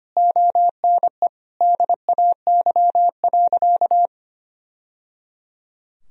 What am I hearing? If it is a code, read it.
Morse code, heard as a keyed tone, about 25 wpm, 700 Hz. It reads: ONE DAY.
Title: Sentences from Top 1000 Words - Review Part 3 - 25wpm